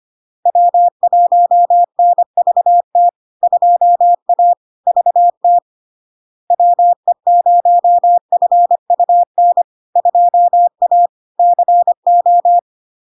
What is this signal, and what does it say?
Morse code, heard as a keyed tone, about 25 wpm, 700 Hz. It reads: W1NVT 2A VT WE0FUN 2A CO
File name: LARK-FD-2022-Log-in-CW-25wpm-700-Hz